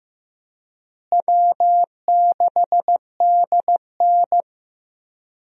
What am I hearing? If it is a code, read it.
Morse code, heard as a keyed tone, about 15 wpm, 700 Hz. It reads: W6DN
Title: USA Call Signs - 4 Characters - Repeated 2x - 15wpm